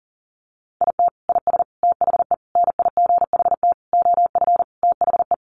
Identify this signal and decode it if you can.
Morse code, heard as a keyed tone, about 40 wpm, 700 Hz. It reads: IT IS THE NIGHT OF THE
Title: Sentences from Top 1400 Words 40wpm